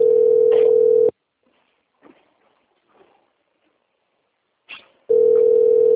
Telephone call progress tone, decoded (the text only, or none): ringback tone